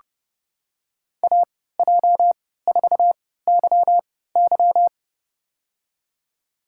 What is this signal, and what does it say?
Morse code, heard as a keyed tone, about 30 wpm, spaced as for 17 wpm, 700 Hz. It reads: AJ4YY
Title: USA Call Signs - 5 Characters - 3x Character Spacing - 30wpm